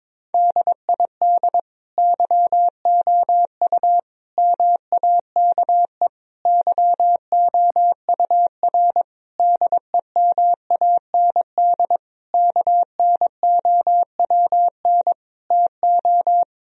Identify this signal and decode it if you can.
Morse code, heard as a keyed tone, about 22 wpm, 700 Hz. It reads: DID YOU MAKE YOUR DEMAND KNOWN TO
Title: Sentences from Top 1000 Words - Review Part 4 - 22wpm